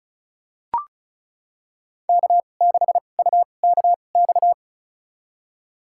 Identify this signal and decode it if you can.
Morse code, heard as a keyed tone, about 34 wpm, spaced as for 25 wpm, 700 Hz. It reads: K6UKX
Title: USA Call Signs - 5 Characters - 2x Character Spacing - 35wpm